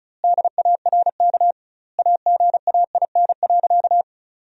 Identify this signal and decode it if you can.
Morse code, heard as a keyed tone, about 34 wpm, 700 Hz. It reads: DARK AGAIN.
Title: 2x Word-Spacing - Sentences from Top 500 Words - 35wpm